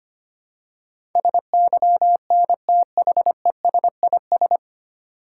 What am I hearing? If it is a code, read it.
Morse code, heard as a keyed tone, about 25 wpm, 690 Hz. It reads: SYNTHESIS
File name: practice / 25wpm / 9 Letter Words - First 3 Letters Given - 25wpm